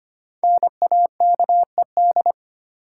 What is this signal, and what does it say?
Morse code, heard as a keyed tone, about 25 wpm, 700 Hz. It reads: NAKED
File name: 5 Letter Words - First 3 Letters Given - 25wpm